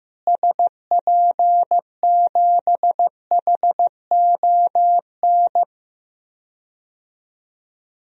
Morse code, heard as a keyed tone, about 15 wpm, 700 Hz. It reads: SP7HON
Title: DXCC-100 Call Signs - 6 Characters - Repeated 2x - 15wpm